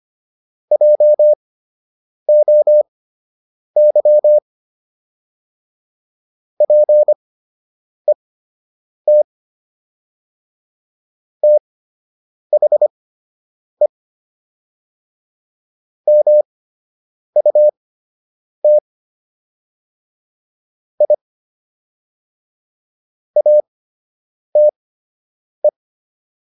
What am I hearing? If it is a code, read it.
Morse code, heard as a keyed tone, about 25 wpm, spaced as for 8 wpm, 600 Hz. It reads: JOY PET THE MUT I ATE